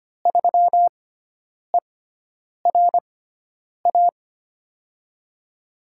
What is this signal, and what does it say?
Morse code, heard as a keyed tone, about 25 wpm, spaced as for 9 wpm, 700 Hz. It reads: 3ERA
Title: USA Call Signs - 6x Character Spacing - 25wpm